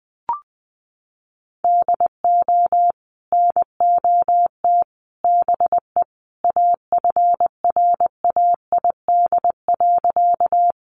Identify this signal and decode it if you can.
Morse code, heard as a keyed tone, about 20 wpm, 700 Hz. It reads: DO NOT BE AFRAID.
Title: Sentences from Top 1500 Words 20wpm